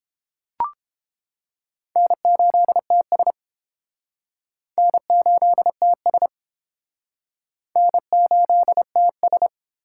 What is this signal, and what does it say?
Morse code, heard as a keyed tone, about 30 wpm, 700 Hz. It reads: N8TS N8TS N8TS